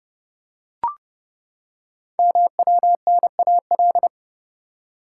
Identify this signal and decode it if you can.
Morse code, heard as a keyed tone, about 28 wpm, 700 Hz. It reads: MWNAL
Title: Code Groups - 5 Letters - 30wpm